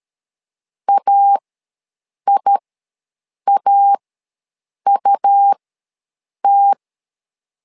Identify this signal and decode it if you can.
Morse code, heard as a keyed tone, about 13 wpm, spaced as for 7 wpm, 800 Hz. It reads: AIAUT
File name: LEZIONI / 2 UHDN / 2 U ws 12wpm 1655